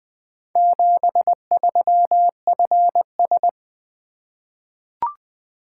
Morse code, heard as a keyed tone, about 20 wpm, 700 Hz. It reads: 73FS
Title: Four Letter-Number Combinations - 20wpm